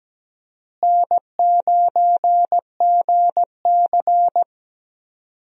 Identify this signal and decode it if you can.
Morse code, heard as a keyed tone, about 17 wpm, 700 Hz. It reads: N9GC
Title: USA Call Signs - 4 Characters - Repeated 3x - 17wpm